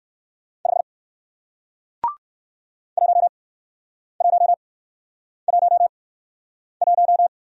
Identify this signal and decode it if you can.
Morse code, heard as a keyed tone, about 60 wpm, spaced as for 9 wpm, 700 Hz. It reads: 5 1111